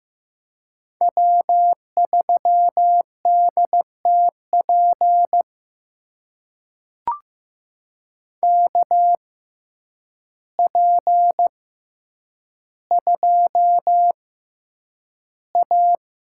Morse code, heard as a keyed tone, about 15 wpm, spaced as for 8 wpm, 700 Hz. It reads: W3DTP KP2A
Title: USA Call Signs - 5 Characters - 6x Character Spacing - 15wpm